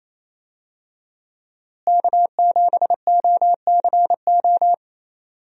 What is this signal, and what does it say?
Morse code, heard as a keyed tone, about 28 wpm, 700 Hz. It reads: K7OCO
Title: USA Call Signs - 5 Characters - Repeated 3x - 28wpm